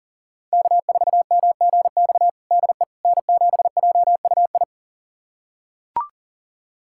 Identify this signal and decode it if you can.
Morse code, heard as a keyed tone, about 37 wpm, 700 Hz. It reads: K4MGX DE N7JUI
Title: USA Callsign DE Callsign - 5 Characters - 40wpm